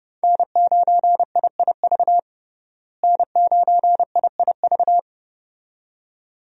Morse code, heard as a keyed tone, about 30 wpm, 700 Hz. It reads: N9IIV N9IIV
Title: USA Call Signs - 5 Characters - Repeated 2x - 30wpm